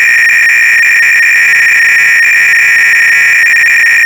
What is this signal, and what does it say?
300-baud modem data.